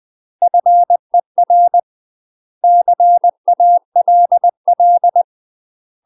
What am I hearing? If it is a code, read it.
Morse code, heard as a keyed tone, about 20 wpm, 700 Hz. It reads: FER CALL
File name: UK MASTER QSOs TRAINING / 20WPM / UK MASTER QSOs 0010